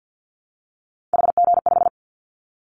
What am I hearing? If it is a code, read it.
Morse code, heard as a keyed tone, about 50 wpm, 700 Hz. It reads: HG5
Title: Callsign Prefixes for Top 50 DXCC - 50wpm